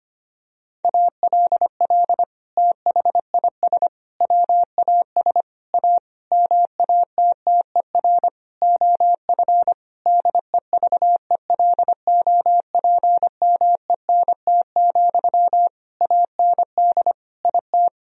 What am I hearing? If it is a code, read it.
Morse code, heard as a keyed tone, about 25 wpm, 700 Hz. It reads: ALL THIS WAS A MATTER OF DEVELOPMENT, AND IT
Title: Sentences from Top 1000 Words - Review Part 4 - 25wpm